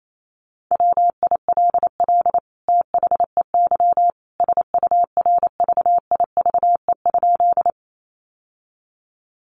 Morse code, heard as a keyed tone, about 28 wpm, 700 Hz. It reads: WILL THEY SURVIVE?
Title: Sentences from Top 1300 Words 28wpm